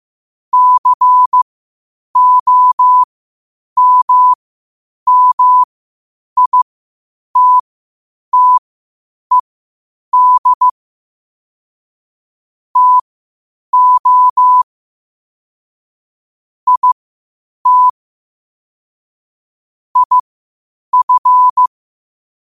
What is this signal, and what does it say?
Morse code, heard as a keyed tone, about 15 wpm, spaced as for 8 wpm, 1000 Hz. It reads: COMMITTED TO IT IF
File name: qotdfiles / 8wpm